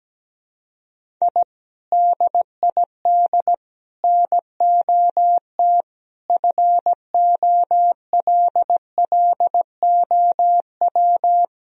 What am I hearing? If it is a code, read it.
Morse code, heard as a keyed tone, about 17 wpm, 700 Hz. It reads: I DID NOT FOLLOW
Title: Sentences from Top 1000 Words - Review Part 1 - 17wpm